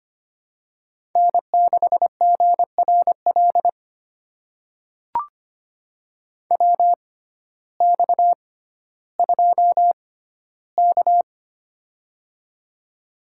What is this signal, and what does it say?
Morse code, heard as a keyed tone, about 25 wpm, spaced as for 14 wpm, 700 Hz. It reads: N6GRL WX2K
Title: USA Call Signs - 6x Character Spacing - 25wpm